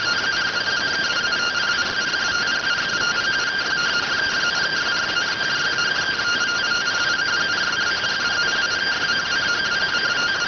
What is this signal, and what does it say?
Two-tone teleprinter signal, about 170 Hz shift.